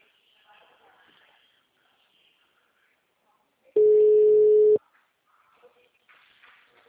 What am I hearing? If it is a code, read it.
Ringback tone (Europe, 425 Hz), with speech.
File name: out- -2019- - - .267